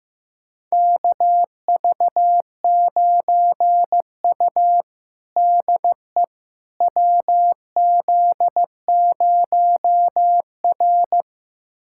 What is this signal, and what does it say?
Morse code, heard as a keyed tone, about 15 wpm, 700 Hz. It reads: KV9U DE WZ0R
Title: USA Callsign DE Callsign - 4 Characters - 15wpm